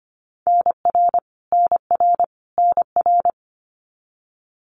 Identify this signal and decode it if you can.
Morse code, heard as a keyed tone, about 25 wpm, 700 Hz. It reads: NR NR NR
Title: Abbreviations Repeated 3x - 25wpm